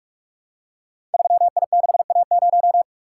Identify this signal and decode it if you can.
Morse code, heard as a keyed tone, about 44 wpm, 690 Hz. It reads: 3I6A0
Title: Code Groups - 5 Letter:Numbers - 45wpm